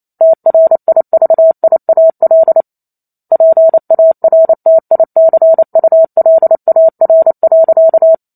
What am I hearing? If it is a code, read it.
Morse code, heard as a keyed tone, about 28 wpm, 650 Hz. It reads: TRIVIAL PARTICULAR.